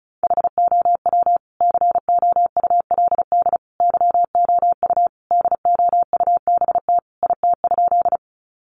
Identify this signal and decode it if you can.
Morse code, heard as a keyed tone, about 34 wpm, 700 Hz. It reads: HOW COULD YOU DOUBT IT?
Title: Sentences from Top 1500 Words 35wpm